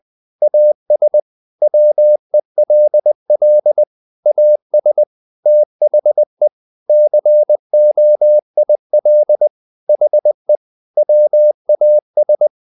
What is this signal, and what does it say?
Morse code, heard as a keyed tone, about 20 wpm, 600 Hz. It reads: AS WELL AS THE COIL HE WAS
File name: the old man and the sea / the old man and the sea part 1 20wpm